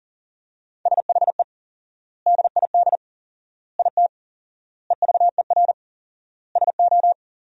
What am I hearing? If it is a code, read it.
Morse code, heard as a keyed tone, about 40 wpm, 700 Hz. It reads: SHE DID IT EVER SO
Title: 4x Word-Spacing - Sentences from Top 400 Words - 40wpm